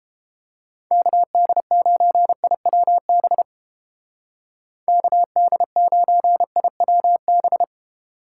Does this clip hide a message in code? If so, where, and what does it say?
Yes: Morse code from 0.9 to 7.6 s, heard as a keyed tone, about 29 wpm, 700 Hz. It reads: KD9IWB KD9IWB